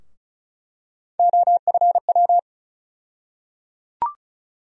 Morse code, heard as a keyed tone, about 36 wpm, 700 Hz. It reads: OFW
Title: USA Call Sign Postfixes - 3 Letters 35wpm